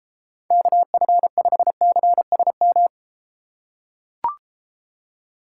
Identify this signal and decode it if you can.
Morse code, heard as a keyed tone, about 34 wpm, 700 Hz. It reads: KF5CSM